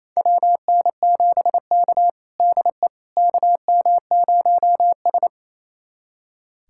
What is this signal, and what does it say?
Morse code, heard as a keyed tone, about 28 wpm, 700 Hz. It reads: WN7K DE KM0S
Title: USA Callsign DE Callsign - 4 Characters - 28wpm